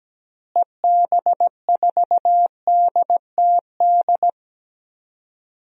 Morse code, heard as a keyed tone, about 17 wpm, 700 Hz. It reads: EB4DTD
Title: DXCC-100 Call Signs - 6 Characters - Repeated 3x - 17wpm